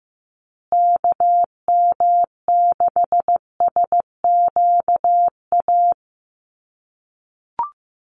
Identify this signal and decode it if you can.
Morse code, heard as a keyed tone, about 15 wpm, 700 Hz. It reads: KM6SQA
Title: USA Call Signs Repeated 2x 15wpm